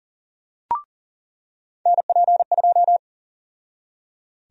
Morse code, heard as a keyed tone, about 40 wpm, 700 Hz. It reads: NP2
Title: USA Call Sign Prefixes - 40wpm